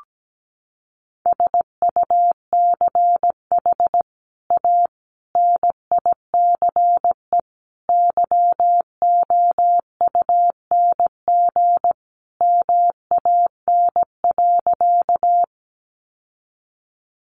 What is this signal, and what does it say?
Morse code, heard as a keyed tone, about 17 wpm, 700 Hz. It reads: SUCH A NICE YOUNG MAN.
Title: Sentences from Top 1100 Words 17wpm